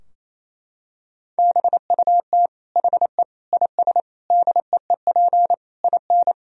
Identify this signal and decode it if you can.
Morse code, heard as a keyed tone, about 28 wpm, 700 Hz. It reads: BUT HE IS DEEP IN
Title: Sentences from Top 1000 Words - Review Part 3 - 28wpm